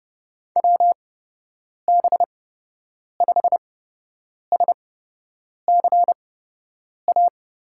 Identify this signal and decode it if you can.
Morse code, heard as a keyed tone, about 30 wpm, spaced as for 8 wpm, 700 Hz. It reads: WB5SCA
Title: USA Call Signs - 6 Characters - 8x Character Spacing - 30wpm